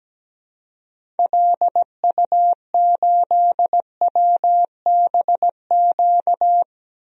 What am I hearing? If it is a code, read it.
Morse code, heard as a keyed tone, about 17 wpm, 700 Hz. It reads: LU8WBQ
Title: DXCC-100 Call Signs - 6 Characters - 17wpm